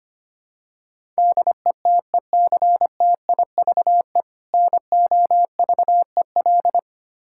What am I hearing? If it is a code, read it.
Morse code, heard as a keyed tone, about 25 wpm, 700 Hz. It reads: DETECTIVE NOVEL
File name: practice / 25wpm / Difficult Words with Hints - 25wpm